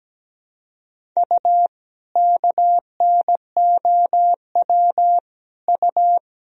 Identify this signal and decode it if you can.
Morse code, heard as a keyed tone, about 17 wpm, 700 Hz. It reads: U KNOW U
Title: Sentences from Top 500 Words and Abbreviations 17wpm